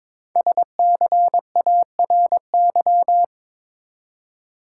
Morse code, heard as a keyed tone, about 22 wpm, 700 Hz. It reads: SCARY
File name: 5 Letter Words - First 3 Letters Given - 22wpm